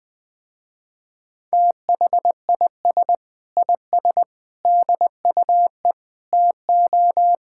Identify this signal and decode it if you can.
Morse code, heard as a keyed tone, about 19 wpm, 700 Hz. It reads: THIS IS DUE TO
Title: Sentences with 8-Word Phrases - Spoken Beginning - 20wpm